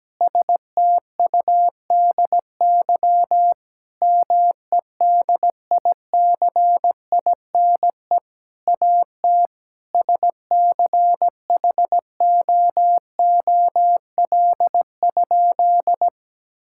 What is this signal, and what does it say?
Morse code, heard as a keyed tone, about 17 wpm, 700 Hz. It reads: STUDY MEDICINE AT SCHOOL?
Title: Sentences from Top 1000 Words - Review Part 6 - 17wpm